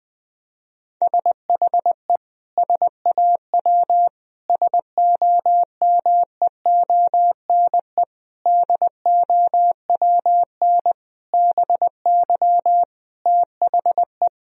Morse code, heard as a keyed tone, about 21 wpm, 700 Hz. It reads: SHE SAW SOMEONE DOWN BY THE
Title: Sentences from Top 1000 Words - Review Part 4 - 20wpm